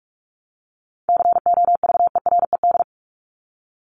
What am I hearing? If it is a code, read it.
Morse code, heard as a keyed tone, about 44 wpm, 700 Hz. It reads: COVERED
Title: ED Suffix - Top 100 Words with Base Words 45wpm